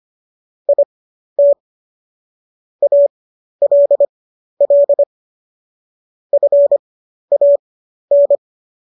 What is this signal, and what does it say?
Morse code, heard as a keyed tone, about 25 wpm, spaced as for 12 wpm, 570 Hz. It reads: IT ALL FAN